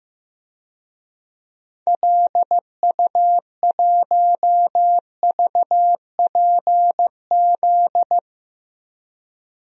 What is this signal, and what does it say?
Morse code, heard as a keyed tone, about 15 wpm, 700 Hz. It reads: LU1VPZ
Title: DXCC-100 Call Signs - 6 Characters - Repeated 2x - 15wpm